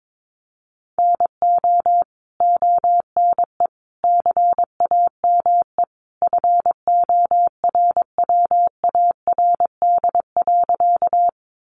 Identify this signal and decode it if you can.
Morse code, heard as a keyed tone, about 22 wpm, 700 Hz. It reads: NO ONE CAME FORWARD.
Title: Sentences from Top 1300 Words 22wpm